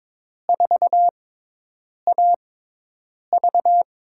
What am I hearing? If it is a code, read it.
Morse code, heard as a keyed tone, about 22 wpm, spaced as for 8 wpm, 700 Hz. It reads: 4AV
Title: USA Call Signs - 6x Character Spacing - 22wpm